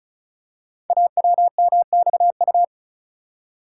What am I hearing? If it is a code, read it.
Morse code, heard as a keyed tone, about 36 wpm, 690 Hz. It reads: AWMXU
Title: Code Groups - 5 Letters - 35wpm